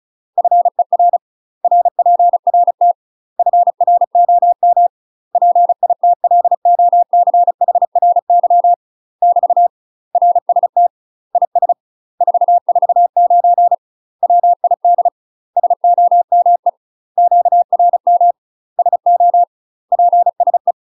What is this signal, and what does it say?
Morse code, heard as a keyed tone, about 34 wpm, 700 Hz. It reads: FER RPRT FROM PITLOCHRY <BT> RST IS 449 WID SOME QRM SO PSE
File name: UK MASTER QSOs TRAINING/35WPM